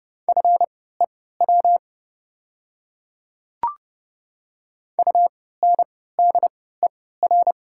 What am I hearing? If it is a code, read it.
Morse code, heard as a keyed tone, about 30 wpm, spaced as for 17 wpm, 700 Hz. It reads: FEW UNDER
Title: 3x Character Spacing - Top 300 Words - 30wpm